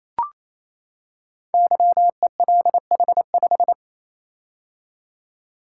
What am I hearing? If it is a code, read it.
Morse code, heard as a keyed tone, about 29 wpm, 700 Hz. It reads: YELH5
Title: Code Groups - 5 Letter:Numbers - 28wpm